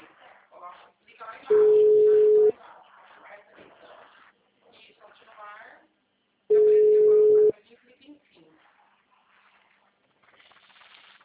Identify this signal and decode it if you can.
Ringback tone (Europe, 425 Hz), with speech.